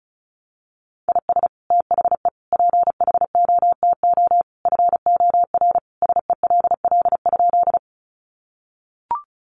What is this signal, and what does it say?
Morse code, heard as a keyed tone, about 34 wpm, 700 Hz. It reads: IS THE PHOTO FOR SELL?
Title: Sentences from Top 1200 Words 35wpm